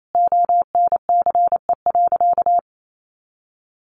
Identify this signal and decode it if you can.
Morse code, heard as a keyed tone, about 28 wpm, 700 Hz. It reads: ONCE.
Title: Sentences from Top 1500 Words 28wpm